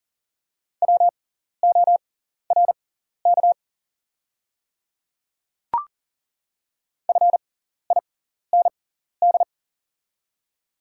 Morse code, heard as a keyed tone, about 40 wpm, spaced as for 14 wpm, 690 Hz. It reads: WORK FIND
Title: 6x Character Spacing - Top 200 Words - 40wpm